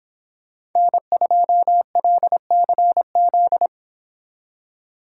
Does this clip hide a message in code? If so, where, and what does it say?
Yes: Morse code from 0.8 to 3.7 s, heard as a keyed tone, about 26 wpm, 700 Hz. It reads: N2LCZ